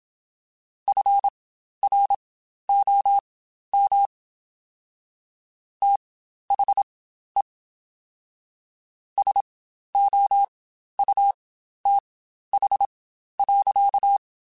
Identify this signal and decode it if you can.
Morse code, heard as a keyed tone, about 27 wpm, spaced as for 12 wpm, 800 Hz. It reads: FROM THE SOUTH.
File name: Spaced Characters / Spaced Characters-mp3s / Weather4 21 sp @25